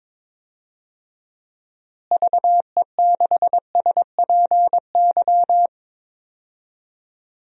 Morse code, heard as a keyed tone, about 22 wpm, 700 Hz. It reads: VE6SPY
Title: DXCC-100 Call Signs - 6 Characters - Repeated 2x - 22wpm